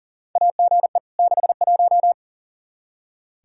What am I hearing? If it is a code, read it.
Morse code, heard as a keyed tone, about 40 wpm, 700 Hz. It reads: AGE 61
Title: Practice QSO - Exchange 3 - More Info - 40wpm